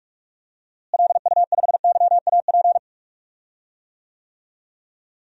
Morse code, heard as a keyed tone, about 44 wpm, 700 Hz. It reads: LU5YAP
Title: DXCC-100 Call Signs - 6 Characters - Repeated 3x - 45wpm